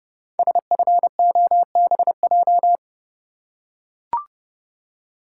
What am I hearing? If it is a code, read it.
Morse code, heard as a keyed tone, about 28 wpm, 700 Hz. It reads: SFOBJ
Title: Code Groups - 5 Letters - 30wpm